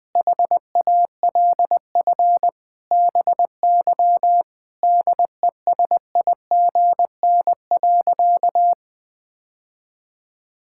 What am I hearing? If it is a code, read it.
Morse code, heard as a keyed tone, about 21 wpm, 700 Hz. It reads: HALF BY DESIGN.
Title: Sentences from Top 1000 Words - Review Part 5 - 20wpm